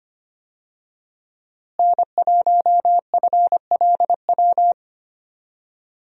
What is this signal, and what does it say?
Morse code, heard as a keyed tone, about 25 wpm, 700 Hz. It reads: N1FLW
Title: USA Call Signs - 5 Characters - Repeated 3x - 25wpm